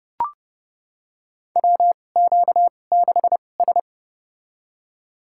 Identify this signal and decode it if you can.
Morse code, heard as a keyed tone, about 30 wpm, spaced as for 22 wpm, 700 Hz. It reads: WQ6S
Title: USA Call Signs - 4 Characters - 2x Character Spacing - 30wpm